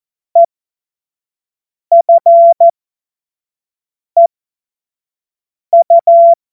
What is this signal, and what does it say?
Morse code, heard as a keyed tone, about 14 wpm, spaced as for 5 wpm, 680 Hz. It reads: EFEU
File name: Morsekurs / Witze / Witz 14.November 2023 25 F70